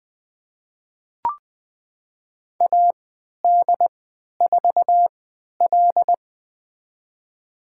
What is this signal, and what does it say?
Morse code, heard as a keyed tone, about 20 wpm, spaced as for 11 wpm, 700 Hz. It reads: AD4L
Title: USA Call Signs - 3x Character Spacing - 20wpm